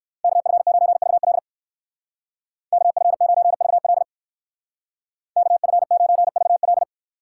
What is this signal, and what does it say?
Morse code, heard as a keyed tone, about 55 wpm, 700 Hz. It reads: KF9VL KF9VL KF9VL
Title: Speed Racing - USA 2x2 Call Signs 45wpm